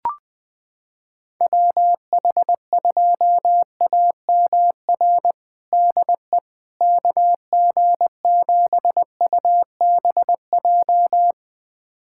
Morse code, heard as a keyed tone, about 20 wpm, 700 Hz. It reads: WH2AMR DE KG7UBJ
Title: USA Callsign DE Callsign - 6 Characters - 20wpm